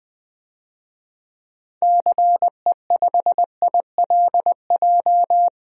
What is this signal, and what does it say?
Morse code, heard as a keyed tone, about 20 wpm, 700 Hz. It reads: CE5ILJ
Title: DXCC-100 Call Signs - 6 Characters - 20wpm